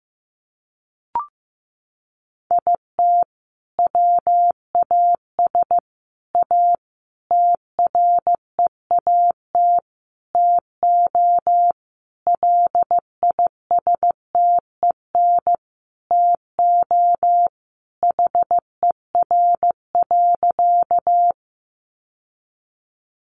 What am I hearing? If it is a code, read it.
Morse code, heard as a keyed tone, about 15 wpm, 700 Hz. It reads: IT WAS A TREAT TO LISTEN TO HER.
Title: Sentences from Top 1300 Words 15wpm